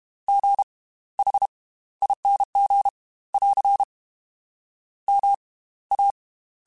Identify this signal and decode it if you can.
Morse code, heard as a keyed tone, about 32 wpm, spaced as for 17 wpm, 800 Hz. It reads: GHING <AR> MA